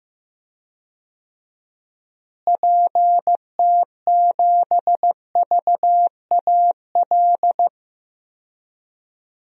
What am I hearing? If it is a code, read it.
Morse code, heard as a keyed tone, about 15 wpm, 700 Hz. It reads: PT7VAL
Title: DXCC-100 Call Signs - 6 Characters - Repeated 3x - 15wpm